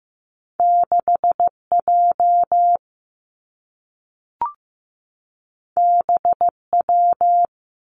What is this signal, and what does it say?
Morse code, heard as a keyed tone, about 15 wpm, 700 Hz. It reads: 6 J B W
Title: Two Challenging Letter-Number Combinations 15wpm